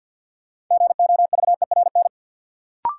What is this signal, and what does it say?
Morse code, heard as a keyed tone, about 52 wpm, 700 Hz. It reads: GOVERN
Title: Sentences from Top 1000 Words - Review Part 2 - New Word Drill - 50wpm